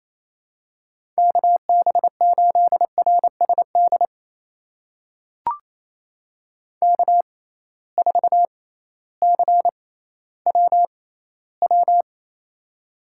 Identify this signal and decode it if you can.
Morse code, heard as a keyed tone, about 28 wpm, spaced as for 15 wpm, 700 Hz. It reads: KB8RSD K4CWW
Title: USA Call Signs - 6x Character Spacing - 28wpm